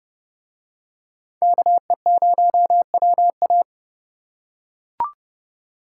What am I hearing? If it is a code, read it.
Morse code, heard as a keyed tone, about 29 wpm, 700 Hz. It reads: KE0WA